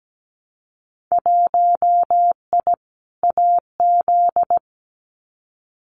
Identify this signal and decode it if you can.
Morse code, heard as a keyed tone, about 17 wpm, 700 Hz. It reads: 1I AZ
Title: Winter Field Day Exchanges 17wpm